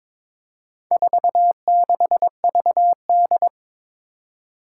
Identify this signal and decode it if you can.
Morse code, heard as a keyed tone, about 22 wpm, 700 Hz. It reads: 46VD
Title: Four Challenging Letter-Number Combinations 22wpm